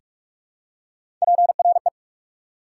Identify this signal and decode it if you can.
Morse code, heard as a keyed tone, about 43 wpm, 690 Hz. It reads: PRE
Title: Prefixes and Suffixes 45wpm